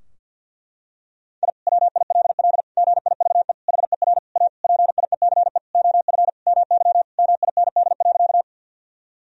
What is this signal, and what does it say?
Morse code, heard as a keyed tone, about 49 wpm, 700 Hz. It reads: I WILL GIVE HER A PIECE OF MY MIND.
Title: Sentences from Top 1000 Words - Review Part 2 - 50wpm